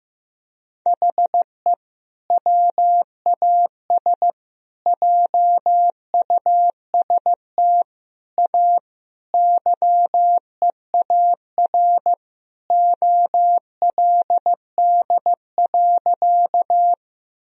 Morse code, heard as a keyed tone, about 16 wpm, 700 Hz. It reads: HE WAS JUST A YEAR OLD.
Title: Sentences from Top 1000 Words - Review Part 1 - 15wpm